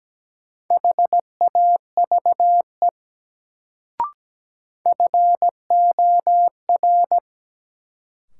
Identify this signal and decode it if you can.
Morse code, heard as a keyed tone, about 17 wpm, 700 Hz. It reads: HAVE FOR
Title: Sets of 2 Words - New Word Drill - 17wpm